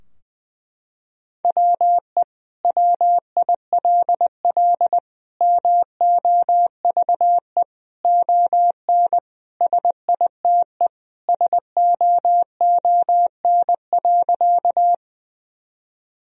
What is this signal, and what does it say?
Morse code, heard as a keyed tone, about 20 wpm, 700 Hz. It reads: WE WILL MOVE ON SITE SOON.
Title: Sentences from Top 1000 Words - Review Part 4 - 20wpm